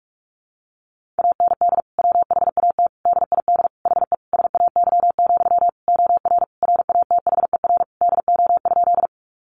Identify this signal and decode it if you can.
Morse code, heard as a keyed tone, about 44 wpm, 700 Hz. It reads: AND WHAT DID HE SAY, OR RATHER DO?
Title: Sentences from Top 700 Words 45wpm